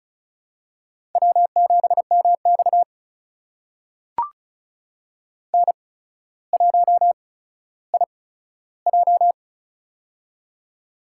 Morse code, heard as a keyed tone, about 36 wpm, spaced as for 15 wpm, 700 Hz. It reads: W7MX N1IJ
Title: USA Call Signs - 4 Characters - 8x Character Spacing - 35wpm